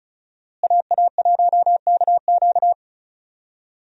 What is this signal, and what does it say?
Morse code, heard as a keyed tone, about 36 wpm, 700 Hz. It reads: AA1KQ
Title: USA Call Signs - 5 Characters - Repeated 3x - 35wpm